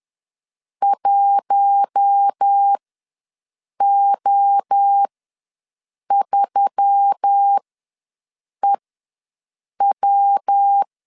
Morse code, heard as a keyed tone, about 10 wpm, spaced as for 6 wpm, 800 Hz. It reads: 1O3EW
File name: LEZIONI / 5 LBVC / 5 L ws 10wpm 3636